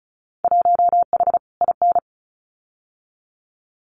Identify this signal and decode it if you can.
Morse code, heard as a keyed tone, about 34 wpm, 700 Hz. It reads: 1H IN
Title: Winter Field Day Exchanges 35wpm